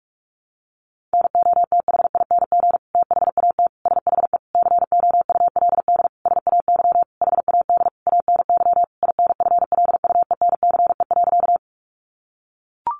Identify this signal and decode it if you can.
Morse code, heard as a keyed tone, about 44 wpm, 700 Hz. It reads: NOTHING THAT SHE COULD SAY HAD ANY INFLUENCE.
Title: Sentences from Top 1100 Words 45wpm